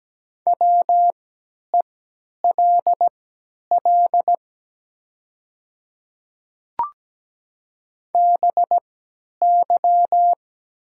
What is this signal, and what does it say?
Morse code, heard as a keyed tone, about 17 wpm, spaced as for 10 wpm, 700 Hz. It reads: WELL BY
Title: 3x Character Spacing - Top 200 Words - 17wpm